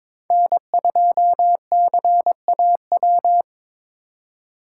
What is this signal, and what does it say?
Morse code, heard as a keyed tone, about 26 wpm, 700 Hz. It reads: N2CAW
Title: Speed Racing - USA 1x3 Call Signs 22wpm